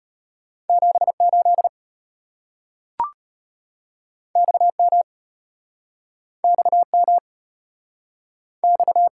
Morse code, heard as a keyed tone, about 34 wpm, 700 Hz. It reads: 78 XM XM X